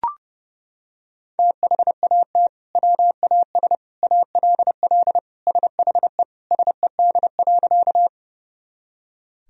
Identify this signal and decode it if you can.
Morse code, heard as a keyed tone, about 30 wpm, 700 Hz. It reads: THAT WAS ALL SHE SED.
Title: Sentences from Top 100 Words and Abbreviations 30wpm